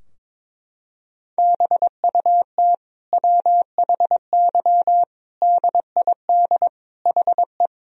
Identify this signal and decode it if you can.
Morse code, heard as a keyed tone, about 22 wpm, 700 Hz. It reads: BUT WHY DID HE
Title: Sentences from Top 1000 Words - Review Part 2 - 22wpm